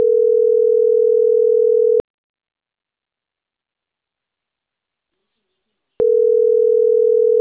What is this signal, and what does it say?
Ringback tone (North America), 440 + 480 Hz.